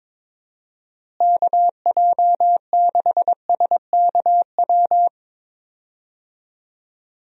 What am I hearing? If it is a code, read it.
Morse code, heard as a keyed tone, about 22 wpm, 700 Hz. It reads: KJ6SKW
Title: USA Call Signs - 6 Characters - Repeated 3x - 22wpm